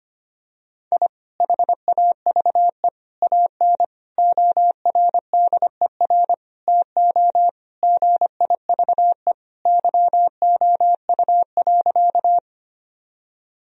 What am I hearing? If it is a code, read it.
Morse code, heard as a keyed tone, about 25 wpm, 700 Hz. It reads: I HAVE AN ORDER TO GIVE YOU.
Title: Sentences from Top 1000 Words - Review Part 2 - 25wpm